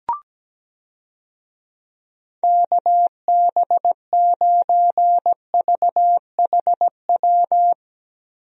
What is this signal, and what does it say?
Morse code, heard as a keyed tone, about 17 wpm, 700 Hz. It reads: KB9VHW
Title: USA Call Signs - 6 Characters - Repeated 2x - 17wpm